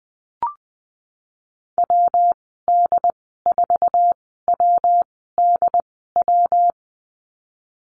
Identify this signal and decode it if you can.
Morse code, heard as a keyed tone, about 20 wpm, spaced as for 14 wpm, 700 Hz. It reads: WD4WDW
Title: 122 Lesson 20g - Number 3 - Call Signs 20wpm